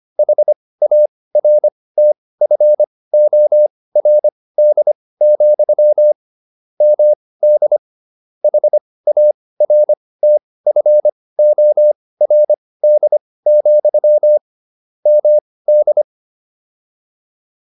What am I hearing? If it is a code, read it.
Morse code, heard as a keyed tone, about 25 wpm, spaced as for 18 wpm, 600 Hz. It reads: HARTFORD, MD HARTFORD, MD